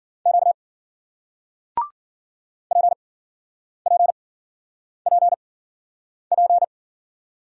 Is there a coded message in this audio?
Yes: Morse code from 0.3 to 6.6 s, heard as a keyed tone, about 54 wpm, spaced as for 9 wpm, 700 Hz. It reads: Q PPPP